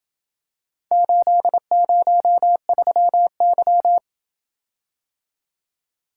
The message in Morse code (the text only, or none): 803Y